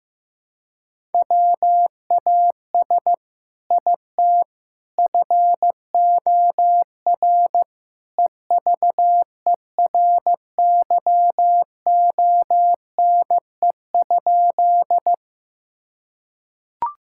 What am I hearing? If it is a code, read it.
Morse code, heard as a keyed tone, about 16 wpm, 700 Hz. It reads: WAS IT FOR EVERYONE?
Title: Sentences from Top 1000 Words - Review Part 5 - 15wpm